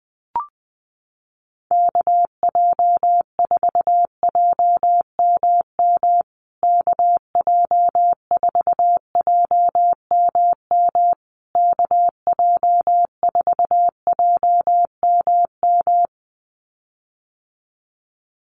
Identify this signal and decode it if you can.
Morse code, heard as a keyed tone, about 20 wpm, 700 Hz. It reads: KJ4JMM KJ4JMM KJ4JMM
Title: USA Call Signs Repeated 3x 20wpm